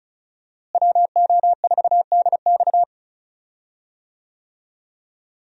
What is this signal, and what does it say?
Morse code, heard as a keyed tone, about 36 wpm, 700 Hz. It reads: WO4DX
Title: USA Call Signs - 5 Characters - Repeated 2x - 35wpm